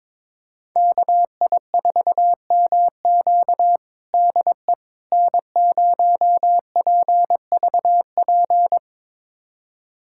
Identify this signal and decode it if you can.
Morse code, heard as a keyed tone, about 22 wpm, 700 Hz. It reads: KI4MQ DE N0PVP
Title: USA Callsign DE Callsign - 5 Characters - 22wpm